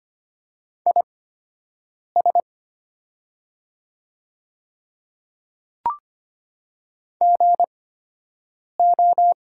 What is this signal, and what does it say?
Morse code, heard as a keyed tone, about 25 wpm, spaced as for 7 wpm, 700 Hz. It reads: IS GO
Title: 8x Character Spacing - Top 100 Words - 25wpm